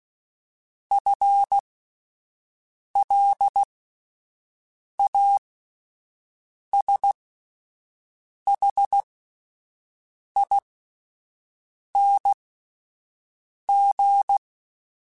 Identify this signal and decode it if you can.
Morse code, heard as a keyed tone, about 16 wpm, spaced as for 5 wpm, 800 Hz. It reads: FLASHING
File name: Spaced / FL Runs sp mp3s / FL Runs sp @10